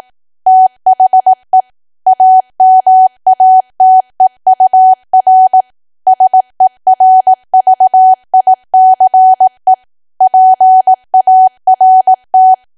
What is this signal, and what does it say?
Morse code, heard as a keyed tone, about 18 wpm, 750 Hz. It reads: THE AMATEUR SERVICE PART
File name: cwa / W1AW-Level2 / 18WPM